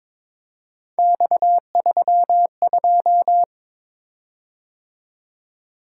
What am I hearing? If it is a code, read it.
Morse code, heard as a keyed tone, about 22 wpm, 700 Hz. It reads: X32
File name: practice / 22wpm / Three Letter-Number Combinations - 22wpm